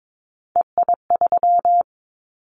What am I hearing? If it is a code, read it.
Morse code, heard as a keyed tone, about 22 wpm, 700 Hz. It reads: EI3
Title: Callsign Prefixes for Top 50 DXCC - 22wpm